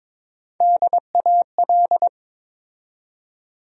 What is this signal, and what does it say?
Morse code, heard as a keyed tone, about 22 wpm, 700 Hz. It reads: DAL
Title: USA Call Sign Postfixes - 3 Letters 22wpm